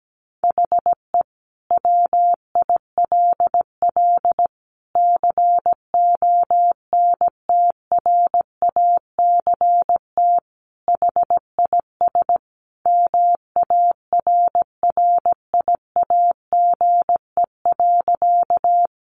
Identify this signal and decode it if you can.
Morse code, heard as a keyed tone, about 17 wpm, 700 Hz. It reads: HE WILL CONTRACT HIS MARRIAGE.
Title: Sentences from Top 1400 Words 17wpm